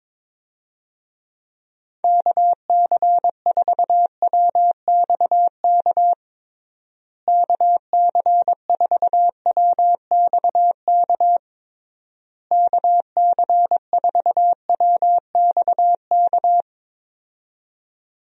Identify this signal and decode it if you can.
Morse code, heard as a keyed tone, about 22 wpm, 700 Hz. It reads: KC4WXK KC4WXK KC4WXK
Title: USA Call Signs - 6 Characters - Repeated 3x - 22wpm